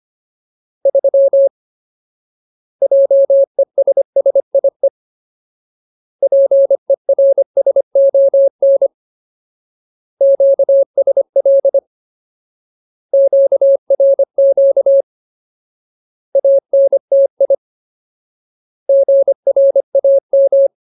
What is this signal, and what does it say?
Morse code, heard as a keyed tone, about 25 wpm, 550 Hz. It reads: 3 JESSIE PERSON QSL QRQ ANTI GRAM
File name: Ultimate Word List / 25WPM Ultimate